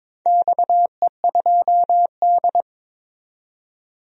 Morse code, heard as a keyed tone, about 22 wpm, 700 Hz. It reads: XE2D
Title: DXCC-100 Call Signs - 4 Characters - 22wpm